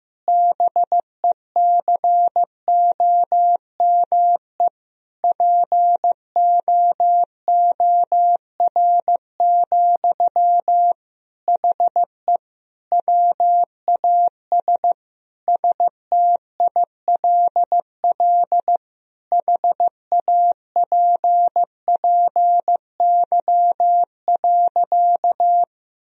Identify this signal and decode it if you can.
Morse code, heard as a keyed tone, about 15 wpm, 700 Hz. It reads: BECOME POOR, HE WAS STILL HAPPY.
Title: Sentences from Top 1000 Words - Review Part 3 - 15wpm